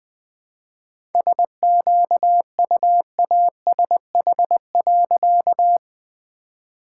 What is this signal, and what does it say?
Morse code, heard as a keyed tone, about 20 wpm, 700 Hz. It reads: SQUASH.
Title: All-Age Appropriate Jokes - 20wpm